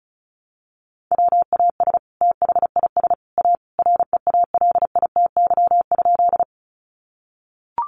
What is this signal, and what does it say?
Morse code, heard as a keyed tone, about 34 wpm, 700 Hz. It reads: WAS THIS A REALITY?
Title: Sentences from Top 1200 Words 35wpm